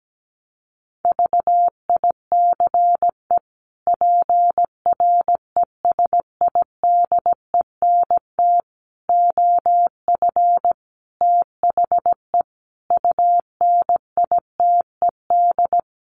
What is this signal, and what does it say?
Morse code, heard as a keyed tone, about 17 wpm, 700 Hz. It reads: VICE PRESIDENT OF THE UNITED
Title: Sets of 5 Words Encore2 17wpm